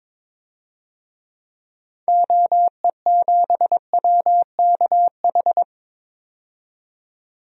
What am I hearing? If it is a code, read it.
Morse code, heard as a keyed tone, about 22 wpm, 700 Hz. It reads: OE7WKH
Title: DXCC-100 Call Signs - 6 Characters - 22wpm